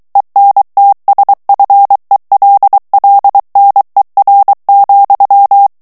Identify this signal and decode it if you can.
Morse code, heard as a keyed tone, about 23 wpm, 800 Hz. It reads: ENTSFELLNER,
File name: morse mp3 / ALLEMAND texte clair 1 vitesse 22 mots minute